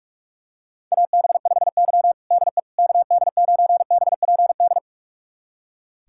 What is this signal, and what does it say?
Morse code, heard as a keyed tone, about 46 wpm, 700 Hz. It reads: AB5Y DE KD9BPD
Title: USA Callsign DE Callsign 45wpm